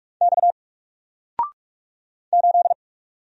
Morse code, heard as a keyed tone, about 44 wpm, 700 Hz. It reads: X 8
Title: Single Character (Letter:Number:Punctuation:Prosign) 45wpm